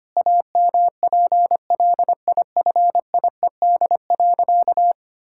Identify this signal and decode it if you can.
Morse code, heard as a keyed tone, about 25 wpm, 700 Hz. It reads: AMPLIFIED.
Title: Pangram Challenge v2 - 25wpm